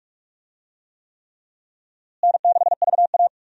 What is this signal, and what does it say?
Morse code, heard as a keyed tone, about 46 wpm, 700 Hz. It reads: N6VA
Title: USA Call Signs - 4 Characters - Repeated 3x - 45wpm